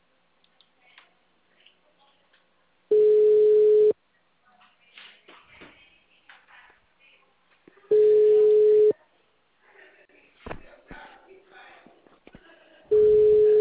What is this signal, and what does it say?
Ringback tone (Europe, 425 Hz).
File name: out- -2057- - - .15896